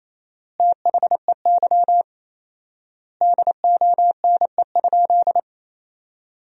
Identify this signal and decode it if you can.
Morse code, heard as a keyed tone, about 28 wpm, 700 Hz. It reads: THEY DONE?
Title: 4x Word-Spacing - Sentences from Top 500 Words - 28wpm